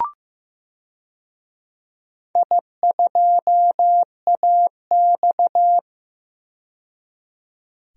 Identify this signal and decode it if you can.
Morse code, heard as a keyed tone, about 15 wpm, 700 Hz. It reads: I2AX
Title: DXCC-100 Call Signs - 4 Characters - 15wpm